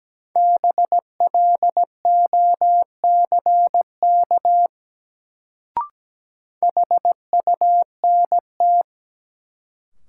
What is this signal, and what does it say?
Morse code, heard as a keyed tone, about 17 wpm, 700 Hz. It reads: BLOCK HUNT
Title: Sentences from Top 1600 Words - New Word Drill Alt - 17wpm